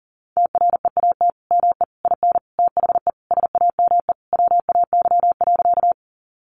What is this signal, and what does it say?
Morse code, heard as a keyed tone, about 40 wpm, 700 Hz. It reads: TREAT ME IN THE SAME WAY.
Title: Sentences from Top 1300 Words 40wpm